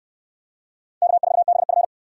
Morse code, heard as a keyed tone, about 63 wpm, 700 Hz. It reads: K4BF